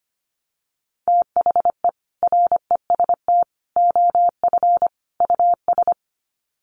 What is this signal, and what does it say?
Morse code, heard as a keyed tone, about 25 wpm, 700 Hz. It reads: THE REST OF US
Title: Sets of 4 Words 25wpm